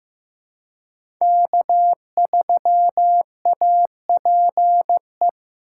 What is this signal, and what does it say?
Morse code, heard as a keyed tone, about 15 wpm, 700 Hz. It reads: K3APE
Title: USA Call Signs - 5 Characters - 3x Character Spacing - 15wpm